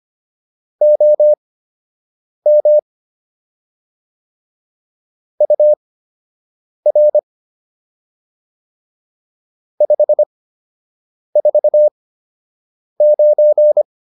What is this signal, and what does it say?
Morse code, heard as a keyed tone, about 25 wpm, spaced as for 7 wpm, 600 Hz. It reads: OM UR 549